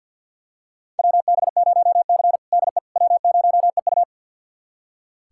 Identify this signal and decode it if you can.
Morse code, heard as a keyed tone, about 46 wpm, 700 Hz. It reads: WB0C DE W0EU
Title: USA Callsign DE Callsign - 4 Characters - 50wpm